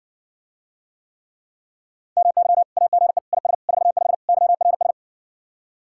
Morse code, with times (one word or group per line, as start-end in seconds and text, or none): 2.2-2.6 MY
2.8-3.2 AGE
3.3-3.5 IS
3.7-4.1 45
4.3-4.9 YRS